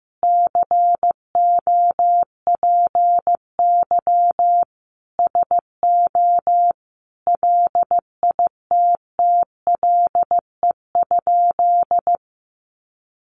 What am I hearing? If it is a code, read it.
Morse code, heard as a keyed tone, about 15 wpm, 700 Hz. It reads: COPY SO LITTLE?
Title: Sentences from Top 1400 Words 15wpm